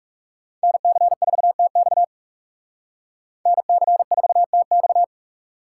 Morse code, heard as a keyed tone, about 38 wpm, 700 Hz. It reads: NC4TX NC4TX